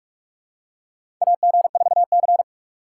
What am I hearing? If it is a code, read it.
Morse code, heard as a keyed tone, about 48 wpm, 700 Hz. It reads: AG4C
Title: USA Call Signs - 4 Characters - 2x Character Spacing - 45wpm